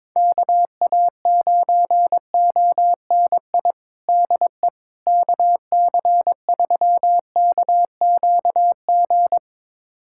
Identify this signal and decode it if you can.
Morse code, heard as a keyed tone, about 22 wpm, 700 Hz. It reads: KA9ONI DE KC3KQG
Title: USA Callsign DE Callsign - 6 Characters - 22wpm